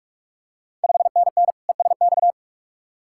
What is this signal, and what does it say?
Morse code, heard as a keyed tone, about 46 wpm, 700 Hz. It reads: 5NN ESX
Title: Ontario QSO Party Exchange 45wpm